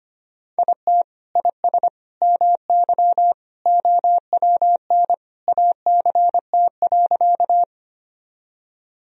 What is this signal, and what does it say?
Morse code, heard as a keyed tone, about 25 wpm, 700 Hz. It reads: IT IS MY OWN ACT.
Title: Sentences from Top 1000 Words - Review Part 1 - 25wpm